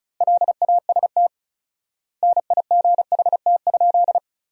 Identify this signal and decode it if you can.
Morse code, heard as a keyed tone, about 34 wpm, 700 Hz. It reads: LAST NIGHT?
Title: 4x Word-Spacing - Sentences from Top 300 Words - 35wpm